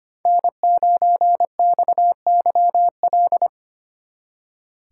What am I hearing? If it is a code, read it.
Morse code, heard as a keyed tone, about 31 wpm, 700 Hz. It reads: N9XYL
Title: Speed Racing - USA 1x3 Call Signs 25wpm